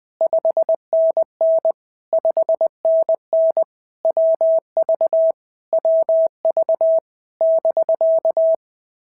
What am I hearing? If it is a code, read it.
Morse code, heard as a keyed tone, about 19 wpm, 650 Hz. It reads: 5NN 5NN WV WV <BK>
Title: POTA QSO Practice - 20wpm